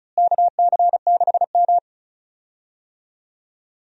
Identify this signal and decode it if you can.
Morse code, heard as a keyed tone, about 36 wpm, 700 Hz. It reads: KC6M
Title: USA Call Signs - 4 Characters - Repeated 3x - 35wpm